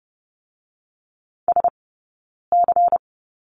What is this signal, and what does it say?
Morse code, heard as a keyed tone, about 30 wpm, 700 Hz. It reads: S C
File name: Lesson 16d - Letter c - Two Characters 30wpm